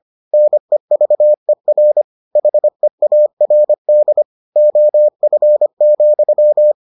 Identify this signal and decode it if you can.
Morse code, heard as a keyed tone, about 25 wpm, 600 Hz. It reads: NEVER HEARD OF,